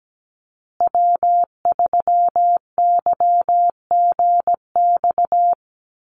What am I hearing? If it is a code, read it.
Morse code, heard as a keyed tone, about 17 wpm, 700 Hz. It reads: W3YGX
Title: USA Call Signs Repeated 3x 17wpm